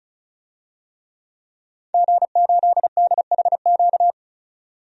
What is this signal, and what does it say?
Morse code, heard as a keyed tone, about 34 wpm, 700 Hz. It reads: G8DHQ
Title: DXCC-100 Call Signs - 5 Characters - 35wpm